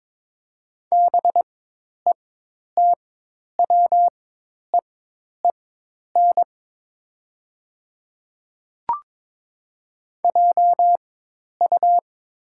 Morse code, heard as a keyed tone, about 22 wpm, spaced as for 10 wpm, 700 Hz. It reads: BETWEEN JU
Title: 4x Character Spacing - Top 500 Words - 22wpm